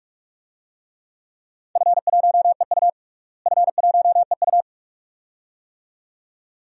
Morse code, heard as a keyed tone, about 46 wpm, 700 Hz. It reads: F1EU F1EU
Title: DXCC-100 Call Signs - 4 Characters - Repeated 2x - 45wpm